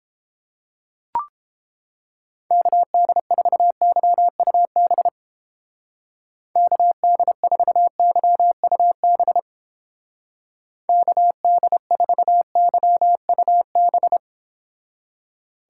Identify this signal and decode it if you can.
Morse code, heard as a keyed tone, about 29 wpm, 700 Hz. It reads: KD4YUB KD4YUB KD4YUB